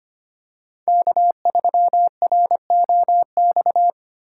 Morse code, heard as a keyed tone, about 25 wpm, 700 Hz. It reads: K3ROX
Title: USA Call Signs - 5 Characters - 4x Character Spacing - 25wpm